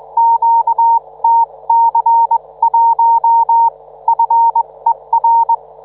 Morse code, heard as a keyed tone, about 21 wpm, spaced as for 18 wpm, 910 Hz. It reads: QTC1 FER